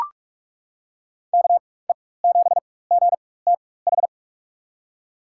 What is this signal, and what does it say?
Morse code, heard as a keyed tone, about 43 wpm, spaced as for 21 wpm, 700 Hz. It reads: KE7GTH
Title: USA Call Signs - 6 Characters - 4x Character Spacing - 45wpm